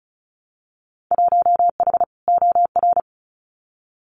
Morse code, heard as a keyed tone, about 34 wpm, 700 Hz. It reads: 1H OR
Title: Winter Field Day Exchanges 35wpm